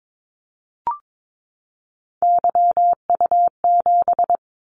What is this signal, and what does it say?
Morse code, heard as a keyed tone, about 22 wpm, 700 Hz. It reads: YU7
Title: Callsign Prefixes for Top 25 DXCC - 22wpm